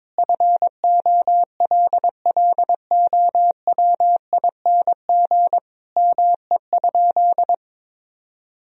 Morse code, heard as a keyed tone, about 21 wpm, 700 Hz. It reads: FOLLOWING ME?
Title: Sentences from Top 1000 Words - Review Part 4 - 22wpm